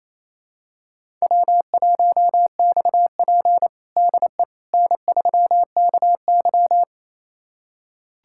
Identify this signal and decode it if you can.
Morse code, heard as a keyed tone, about 28 wpm, 700 Hz. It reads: W1XP DE N3KY
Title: USA Callsign DE Callsign - 4 Characters - 28wpm